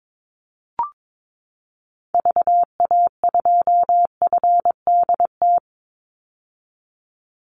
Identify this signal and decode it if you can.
Morse code, heard as a keyed tone, about 22 wpm, 700 Hz. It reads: VA2FDT
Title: Call Signs - 22wpm